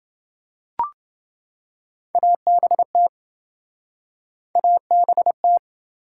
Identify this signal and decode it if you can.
Morse code, heard as a keyed tone, about 28 wpm, 700 Hz. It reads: ABT ABT